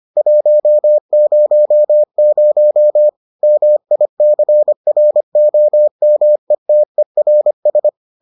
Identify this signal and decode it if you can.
Morse code, heard as a keyed tone, about 25 wpm, 600 Hz. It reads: 100 MICROMETERS